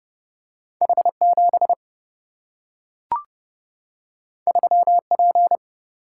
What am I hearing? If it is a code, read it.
Morse code, heard as a keyed tone, about 31 wpm, 700 Hz. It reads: H 7 3 P
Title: Single-Letter Single-Number 30wpm